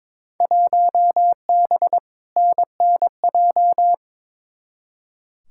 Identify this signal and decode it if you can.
Morse code, heard as a keyed tone, about 23 wpm, 700 Hz. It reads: 1B NNJ
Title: Field Day Exchanges - 22wpm